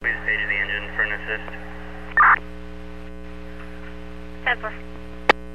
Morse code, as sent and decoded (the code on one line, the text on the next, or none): none